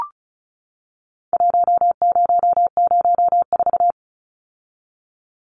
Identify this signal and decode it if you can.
Morse code, heard as a keyed tone, about 35 wpm, 700 Hz. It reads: 1004
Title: Numbers - 4-Digit 35wpm